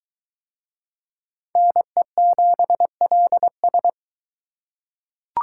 Morse code, heard as a keyed tone, about 21 wpm, 700 Hz. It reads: NE7LS